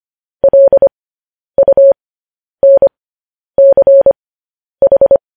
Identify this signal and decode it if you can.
Morse code, heard as a keyed tone, about 25 wpm, spaced as for 10 wpm, 550 Hz. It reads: LUNCH